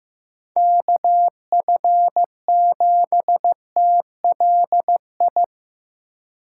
Morse code, heard as a keyed tone, about 18 wpm, 700 Hz. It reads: KF7TLI
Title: Speed Racing - USA 2x3 Call Signs 15wpm